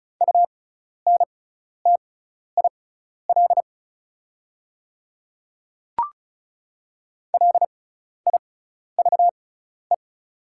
Morse code, heard as a keyed tone, about 36 wpm, spaced as for 12 wpm, 690 Hz. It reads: UNTIL LIVE
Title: 6x Character Spacing - Top 300 Words - 35wpm